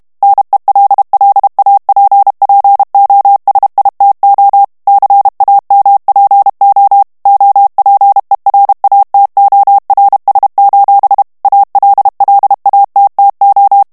Morse code, heard as a keyed tone, about 32 wpm, 800 Hz. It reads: NELLAPPOSITO CAMPO OPERATORS: ALLATTO
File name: ITALIEN texte clair 2 vitesse 30 mots minute